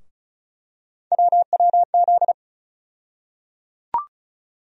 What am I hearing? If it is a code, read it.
Morse code, heard as a keyed tone, about 36 wpm, 700 Hz. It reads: WWZ
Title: USA Call Sign Postfixes - 3 Letters 35wpm